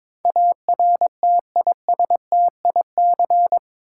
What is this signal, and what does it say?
Morse code, heard as a keyed tone, about 22 wpm, 700 Hz. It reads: ARTISTIC
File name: practice / 22wpm / Eight Letter Words - 22wpm